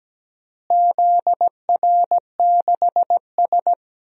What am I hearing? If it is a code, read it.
Morse code, heard as a keyed tone, about 17 wpm, 700 Hz. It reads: ZR6S
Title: DXCC-100 Call Signs - 4 Characters - Repeated 2x - 17wpm